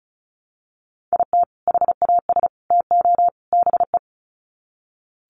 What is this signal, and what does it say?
Morse code, heard as a keyed tone, about 34 wpm, 700 Hz. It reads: IT HAS TO BE
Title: Sets of 4 Words 35wpm